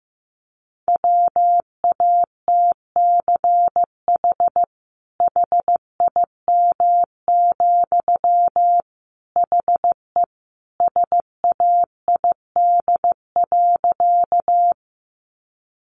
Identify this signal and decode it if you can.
Morse code, heard as a keyed tone, about 15 wpm, 700 Hz. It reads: WATCH HIM, HE SAID.
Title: Sentences from Top 300 Words 15wpm